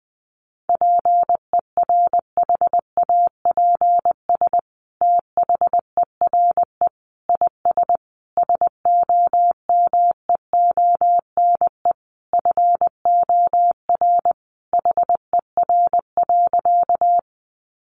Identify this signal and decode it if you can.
Morse code, heard as a keyed tone, about 20 wpm, 700 Hz. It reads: PERHAPS THERE IS SOMEONE FOR HER.
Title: Sentences from Top 600 Words 20wpm